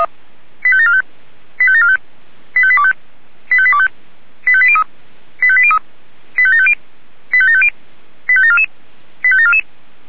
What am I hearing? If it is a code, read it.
Five-tone selective call (ZVEI), heard as quick runs of five tones, tones of about 70 ms. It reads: 85745 85745 85825 85825 85803 85803 85750 85750 85744 85744
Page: ZVEI